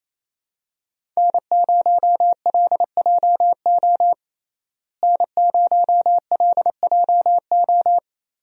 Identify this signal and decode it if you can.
Morse code, heard as a keyed tone, about 28 wpm, 700 Hz. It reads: N0LJO N0LJO
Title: USA Call Signs - 5 Characters - Repeated 3x - 28wpm